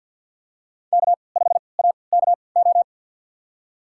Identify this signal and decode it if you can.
Morse code, heard as a keyed tone, about 52 wpm, spaced as for 29 wpm, 700 Hz. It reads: K5AKO
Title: USA Call Signs - 3x Character Spacing - 50wpm